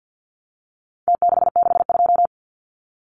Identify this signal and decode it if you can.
Morse code, heard as a keyed tone, about 50 wpm, 700 Hz. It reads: T662
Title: Numbers - Contest Style - 4-Digit 50wpm